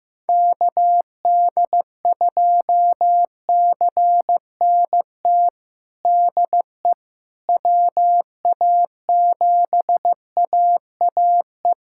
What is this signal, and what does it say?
Morse code, heard as a keyed tone, about 15 wpm, 700 Hz. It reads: KD2CNT DE WA7AAE
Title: USA Callsign DE Callsign - 6 Characters - 15wpm